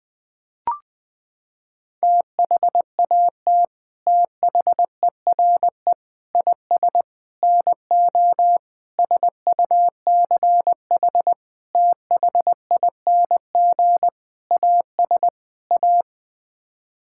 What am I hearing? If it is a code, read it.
Morse code, heard as a keyed tone, about 20 wpm, 700 Hz. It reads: THAT THERE IS NO SUCH THING AS A
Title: Sets of 8 Words - 20wpm